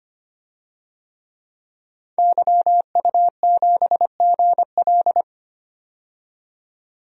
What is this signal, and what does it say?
Morse code, heard as a keyed tone, about 25 wpm, 700 Hz. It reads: YU7GL
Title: DXCC-100 Call Signs - 5 Characters - 25wpm